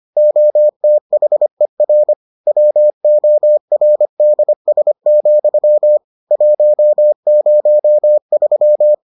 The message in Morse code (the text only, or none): OTHER WORDS, 103